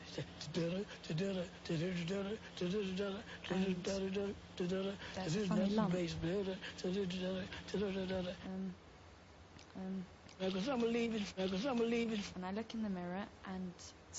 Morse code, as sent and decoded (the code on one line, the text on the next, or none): none